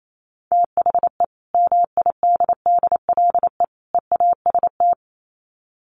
Morse code, heard as a keyed tone, about 28 wpm, 700 Hz. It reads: THE MIDDLE EAST
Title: Sets of 3 Words 28wpm